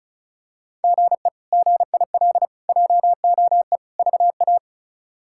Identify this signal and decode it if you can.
Morse code, heard as a keyed tone, about 36 wpm, 700 Hz. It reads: GE GIL JOE VA
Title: K1USN SST QSO Practice - 35wpm